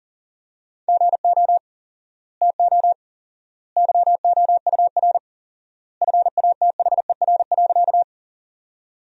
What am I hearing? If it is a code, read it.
Morse code, heard as a keyed tone, about 40 wpm, 700 Hz. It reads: GO TO YOUR FATHER.
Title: 4x Word-Spacing - Sentences from Top 300 Words - 40wpm